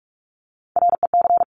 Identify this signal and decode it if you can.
Morse code, heard as a keyed tone, about 46 wpm, 700 Hz. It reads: REC
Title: Top Three Letter Combinations in Words 45wpm